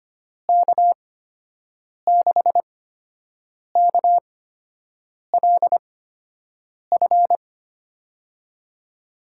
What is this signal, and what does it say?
Morse code, heard as a keyed tone, about 25 wpm, spaced as for 7 wpm, 700 Hz. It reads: K6KLF
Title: USA Call Signs - 6 Characters - 8x Character Spacing - 25wpm